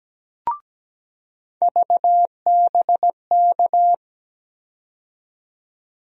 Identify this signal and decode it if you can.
Morse code, heard as a keyed tone, about 17 wpm, 700 Hz. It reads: VBK
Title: USA Call Sign Postfixes - 3 Letters 17wpm